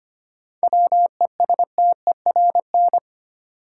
Morse code, heard as a keyed tone, about 25 wpm, 700 Hz. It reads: WESTERN
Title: Sentences from Top 1000 Words - Review Part 7 - New Word Drill - 25wpm